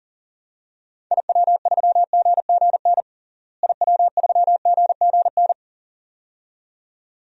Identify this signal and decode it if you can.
Morse code, heard as a keyed tone, about 40 wpm, 700 Hz. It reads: IW3GGN IW3GGN
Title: DXCC-100 Call Signs - 6 Characters - Repeated 2x - 40wpm